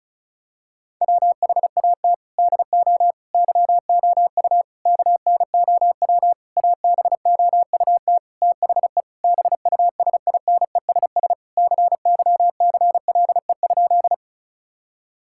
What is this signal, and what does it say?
Morse code, heard as a keyed tone, about 36 wpm, 700 Hz. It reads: WHAT DO YOU KNOW ABOUT THE BUSINESS CYCLE?
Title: Sentences from Top 1000 Words - Review Part 6 - 35wpm